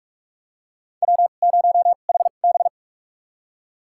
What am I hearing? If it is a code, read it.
Morse code, heard as a keyed tone, about 45 wpm, spaced as for 32 wpm, 700 Hz. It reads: W0HB
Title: USA Call Signs - 4 Characters - 2x Character Spacing - 45wpm